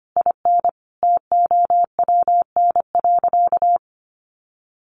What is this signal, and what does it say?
Morse code, heard as a keyed tone, about 25 wpm, 700 Hz. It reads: IN TOWN.
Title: Sentences from Top 1500 Words 25wpm